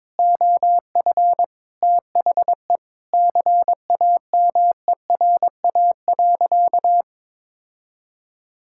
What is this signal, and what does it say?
Morse code, heard as a keyed tone, about 22 wpm, 700 Hz. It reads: OF THE CAMERA.
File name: practice / 22wpm / Sentences with 3-Word Phrases - 22wpm